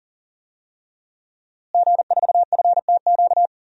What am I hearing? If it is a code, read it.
Morse code, heard as a keyed tone, about 40 wpm, 700 Hz. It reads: G4FTQ
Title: DXCC-100 Call Signs - 5 Characters - Repeated 2x - 40wpm